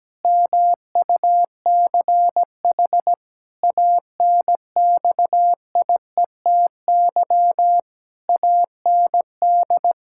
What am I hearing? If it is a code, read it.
Morse code, heard as a keyed tone, about 17 wpm, 700 Hz. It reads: MUCH ANXIETY AND
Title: Pangram Challenge v2 - 17wpm